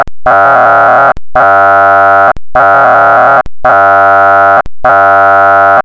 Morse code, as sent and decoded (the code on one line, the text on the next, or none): none